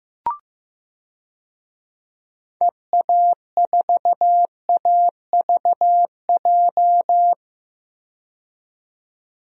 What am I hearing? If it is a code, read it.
Morse code, heard as a keyed tone, about 15 wpm, 700 Hz. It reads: EA4AVJ
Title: DXCC-100 Call Signs - 6 Characters - 15wpm